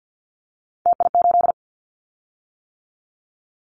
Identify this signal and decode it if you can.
Morse code, heard as a keyed tone, about 50 wpm, 700 Hz. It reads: TI8
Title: Callsign Prefixes for Top 100 DXCC - 50wpm